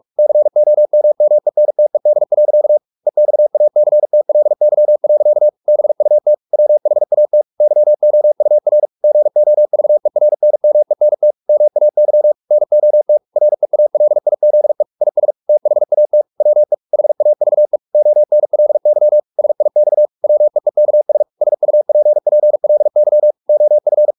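Morse code, heard as a keyed tone, about 44 wpm, 600 Hz. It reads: COMMENTED. EXACTLY. BUT WHAT YOUR GOVERNMENT MAY NOT REALIZE IS THAT WE HAVE ONLY SIX WEEKS SUPPLY OF
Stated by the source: Morse practice at 45 wpm, the novel Never by Ken Follett in English, from CW med Gnister.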